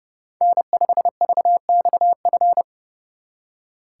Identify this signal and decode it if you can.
Morse code, heard as a keyed tone, about 30 wpm, 700 Hz. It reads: N5VXF
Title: Speed Racing - USA Call Signs 30wpm